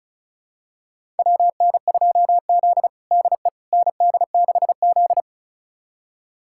Morse code, heard as a keyed tone, about 36 wpm, 700 Hz. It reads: WN2Z DE ND6Z
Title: USA Callsign DE Callsign - 4 Characters - 35wpm